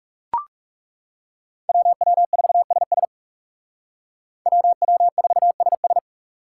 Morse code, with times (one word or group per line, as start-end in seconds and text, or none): 1.7-3.0 WW4SS
4.5-6.0 WW4SS